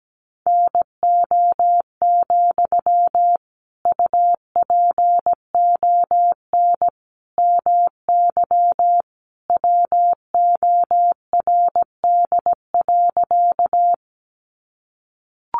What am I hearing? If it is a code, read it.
Morse code, heard as a keyed tone, about 17 wpm, 700 Hz. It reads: NO, UPON MY WORD.
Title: Sentences from Top 1100 Words 17wpm